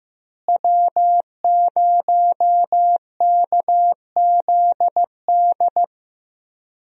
Morse code, heard as a keyed tone, about 15 wpm, 700 Hz. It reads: W0KZD
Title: USA Call Signs - 5 Characters - Repeated 2x - 15wpm